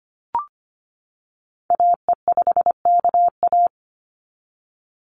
Morse code, heard as a keyed tone, about 25 wpm, 700 Hz. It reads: AE5KA
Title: NA SOTA Activator Callsigns 25wpm